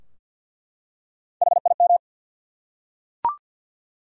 Morse code, heard as a keyed tone, about 50 wpm, 700 Hz. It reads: HIM
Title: Top 100 Words 50wpm